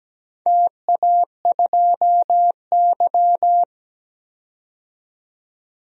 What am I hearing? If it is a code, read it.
Morse code, heard as a keyed tone, about 17 wpm, 700 Hz. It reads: TA2Y
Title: DXCC-100 Call Signs - 4 Characters - 17wpm